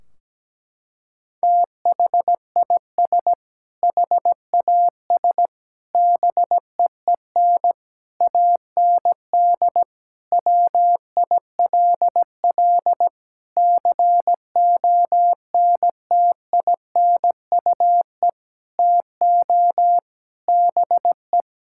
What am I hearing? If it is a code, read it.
Morse code, heard as a keyed tone, about 17 wpm, 700 Hz. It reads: THIS HAS BEEN AND WILL CONTINUE TO BE
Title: Sentences from Top 1600 Words 17wpm